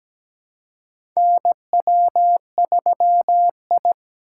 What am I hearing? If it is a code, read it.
Morse code, heard as a keyed tone, about 17 wpm, 700 Hz. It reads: NW3I
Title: USA Call Signs - 4 Characters - 2x Character Spacing - 17wpm